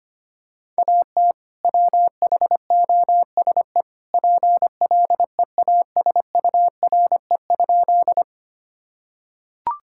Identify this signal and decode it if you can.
Morse code, heard as a keyed tone, about 25 wpm, 700 Hz. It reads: AT WHOSE PLEASURE?
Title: Sentences from Top 1600 Words 25wpm